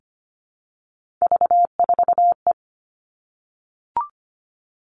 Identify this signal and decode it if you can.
Morse code, heard as a keyed tone, about 25 wpm, 700 Hz. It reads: V4E
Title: Three Challenging Letter-Number Combinations 25wpm